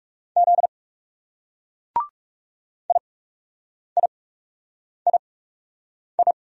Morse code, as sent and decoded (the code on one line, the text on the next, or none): --.. / .. .. .. ..
Z IIII